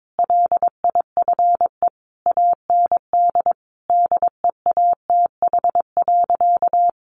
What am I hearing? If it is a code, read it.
Morse code, heard as a keyed tone, about 22 wpm, 700 Hz. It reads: LIFE AND DEATH.
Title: Sentences from Top 1100 Words 22wpm